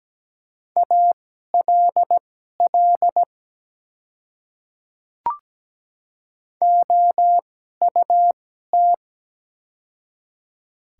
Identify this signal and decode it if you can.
Morse code, heard as a keyed tone, about 17 wpm, spaced as for 12 wpm, 700 Hz. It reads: ALL OUT
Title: 2x Character Spacing - Top 300 Words - 17wpm